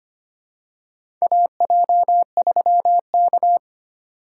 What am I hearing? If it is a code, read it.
Morse code, heard as a keyed tone, about 25 wpm, 700 Hz. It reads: AJ3K
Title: USA Call Signs - 4 Characters - 3x Character Spacing - 25wpm